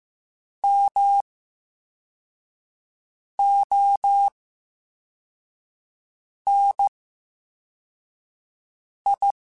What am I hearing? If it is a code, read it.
Morse code, heard as a keyed tone, about 15 wpm, 800 Hz. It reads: M O N I
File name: Special INGs-mp3s / Special INGs-mp3s / VerySlowToFaster Ing @13